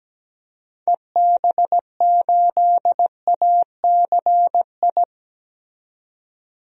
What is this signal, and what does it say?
Morse code, heard as a keyed tone, about 17 wpm, 700 Hz. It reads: EB8ACI
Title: DXCC-100 Call Signs - 6 Characters - 17wpm